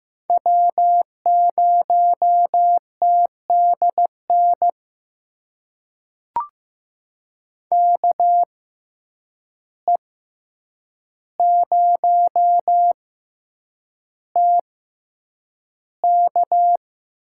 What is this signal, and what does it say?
Morse code, heard as a keyed tone, about 15 wpm, spaced as for 8 wpm, 700 Hz. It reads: W0TDN KE0TK
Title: USA Call Signs - 5 Characters - 6x Character Spacing - 15wpm